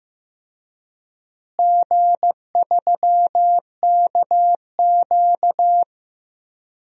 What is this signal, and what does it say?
Morse code, heard as a keyed tone, about 15 wpm, 700 Hz. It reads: G3KQ
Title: DXCC-100 Call Signs - 4 Characters - Repeated 2x - 15wpm